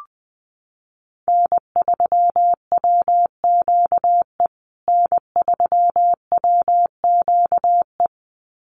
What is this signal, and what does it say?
Morse code, heard as a keyed tone, about 20 wpm, 700 Hz. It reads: N3WQE N3WQE
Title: USA Call Signs Repeated 2x 20wpm